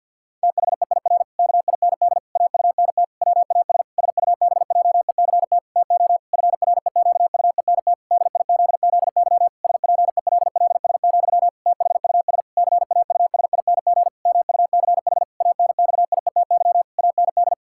Morse code, heard as a keyed tone, about 50 wpm, 700 Hz. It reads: THEIR KIND AUNT WAS SUBJECT TO FREQUENT DIZZY SPELLS, THUS CAUSING MUCH ANXIETY AND
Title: Pangram Challenge v2 - 50wpm